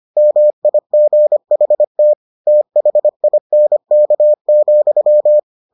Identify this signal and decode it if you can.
Morse code, heard as a keyed tone, about 25 wpm, 600 Hz. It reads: MIGHT THINK,